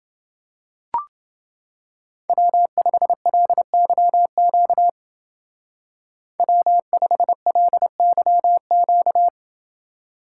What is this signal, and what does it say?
Morse code, heard as a keyed tone, about 28 wpm, 700 Hz. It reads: W5LYQ W5LYQ